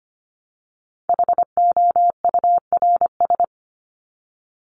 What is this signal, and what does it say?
Morse code, heard as a keyed tone, about 25 wpm, 700 Hz. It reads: HOURS
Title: ES:S Suffix - Top 100 Words with Base Words 25wpm